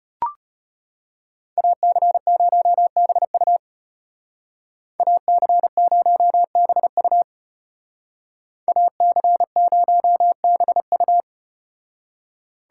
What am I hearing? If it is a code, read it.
Morse code, heard as a keyed tone, about 35 wpm, 700 Hz. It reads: AC0BU AC0BU AC0BU